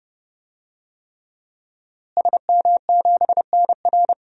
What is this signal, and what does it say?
Morse code, heard as a keyed tone, about 30 wpm, 700 Hz. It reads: SM7NR
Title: DXCC-100 Call Signs - 5 Characters - Repeated 3x - 30wpm